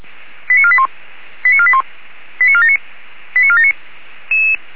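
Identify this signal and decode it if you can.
Five-tone selective call (ZVEI), heard as quick runs of five tones, tones of about 70 ms. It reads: 89481 89481 89479 89479
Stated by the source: ZVEI five-tone call 89134 received with an SDR.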